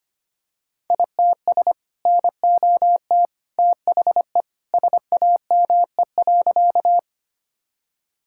Morse code, heard as a keyed tone, about 25 wpm, 700 Hz. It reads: ITS NOT THE SAME.
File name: practice / 25wpm / Sentences with 2-Word Phrases - 25wpm